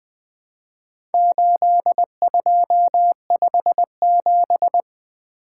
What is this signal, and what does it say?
Morse code, heard as a keyed tone, about 20 wpm, 700 Hz. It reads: 8257
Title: Four Letter-Number Combinations - 20wpm